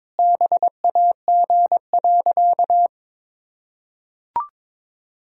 Morse code, heard as a keyed tone, about 21 wpm, 700 Hz. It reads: BAG.
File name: Sentences with 7-Word Phrases - 22wpm